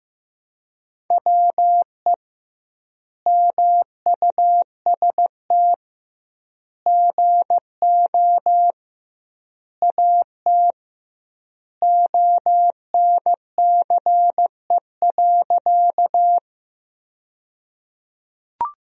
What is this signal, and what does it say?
Morse code, heard as a keyed tone, about 15 wpm, 700 Hz. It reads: WE MUST GO AT ONCE.
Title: 2x Word-Spacing - Sentences from Top 300 Words - 15wpm